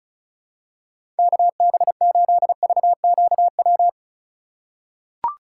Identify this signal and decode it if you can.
Morse code, heard as a keyed tone, about 36 wpm, 700 Hz. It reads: KB8VQW
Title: USA Call Signs - 6 Characters - 3x Character Spacing - 35wpm